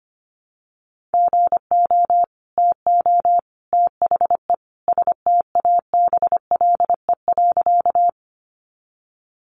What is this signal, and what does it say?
Morse code, heard as a keyed tone, about 25 wpm, 700 Hz. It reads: GO TO THE STABLE.
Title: Sentences from Top 1500 Words 25wpm